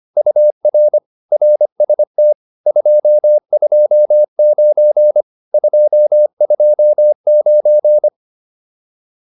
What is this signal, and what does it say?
Morse code, heard as a keyed tone, about 25 wpm, 600 Hz. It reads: UR RST 229 229
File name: QSO407 25